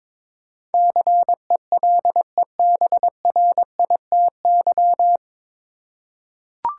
Morse code, heard as a keyed tone, about 22 wpm, 700 Hz. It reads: CELEBRITY
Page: Morse Code Ninja practice 22wpm